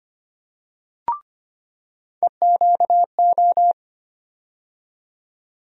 Morse code, heard as a keyed tone, about 25 wpm, 700 Hz. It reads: EQO
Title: USA Call Sign Postfixes - 3 Letters 25wpm